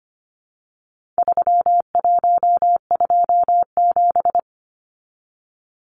Morse code, heard as a keyed tone, about 25 wpm, 700 Hz. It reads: 3127
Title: Numbers - 4-Digit 25wpm